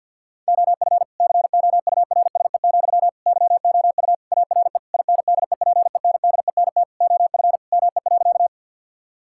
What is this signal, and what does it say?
Morse code, heard as a keyed tone, about 50 wpm, 700 Hz. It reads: OF COURSE, YOU ARE INDEPENDENT OF ME.
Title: Sentences from Top 1000 Words - Review Part 6 - 50wpm